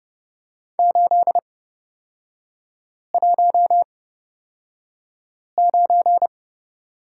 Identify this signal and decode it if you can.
Morse code, heard as a keyed tone, about 30 wpm, spaced as for 5 wpm, 700 Hz. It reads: 819
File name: Lesson 38e - Letter q - Three Characters 30wpm